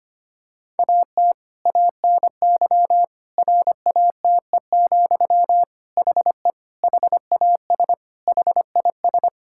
Morse code, heard as a keyed tone, about 25 wpm, 700 Hz. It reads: AT ANY RATE, HE HAS HIS
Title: Sentences from Top 1000 Words - Review Part 5 - 25wpm